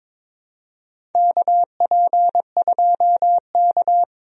Morse code, heard as a keyed tone, about 22 wpm, 700 Hz. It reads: KP2K
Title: DXCC-100 Call Signs - 4 Characters - 22wpm